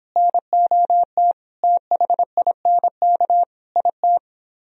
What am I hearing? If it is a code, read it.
Morse code, heard as a keyed tone, about 26 wpm, 700 Hz. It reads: NOT THINK IT